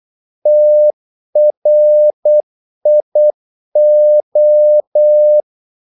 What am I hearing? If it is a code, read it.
Morse code, heard as a keyed tone, about 8 wpm, 600 Hz. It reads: TRIO
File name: Korte ord DK 8wpm